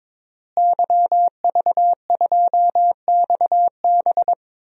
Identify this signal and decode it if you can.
Morse code, heard as a keyed tone, about 21 wpm, 700 Hz. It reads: YV2XB
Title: DXCC-100 Call Signs - 5 Characters - Repeated 2x - 22wpm